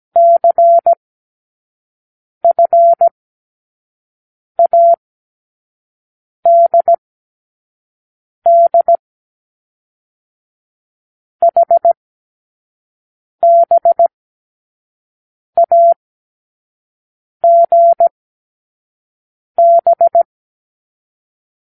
Morse code, heard as a keyed tone, about 17 wpm, 680 Hz. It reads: C F A D D H B A G B
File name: k7qo / Lesson12(A-H)